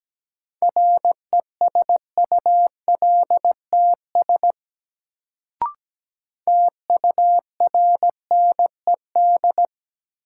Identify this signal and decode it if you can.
Morse code, heard as a keyed tone, about 17 wpm, 700 Hz. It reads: RESULTS TURNED
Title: Sets of 4 Words - New Word Drill - 17wpm